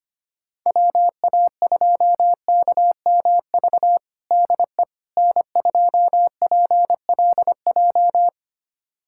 Morse code, heard as a keyed tone, about 25 wpm, 700 Hz. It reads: WA2KMV DE N2PLJ
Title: USA Callsign DE Callsign 25wpm